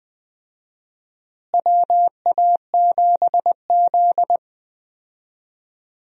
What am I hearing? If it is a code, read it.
Morse code, heard as a keyed tone, about 21 wpm, 700 Hz. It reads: WA7Z
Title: USA Call Signs - 4 Characters - Repeated 2x - 20wpm